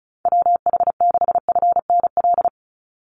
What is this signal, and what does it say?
Morse code, heard as a keyed tone, about 34 wpm, 700 Hz. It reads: WH6FNL
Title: USA Call Signs 35wpm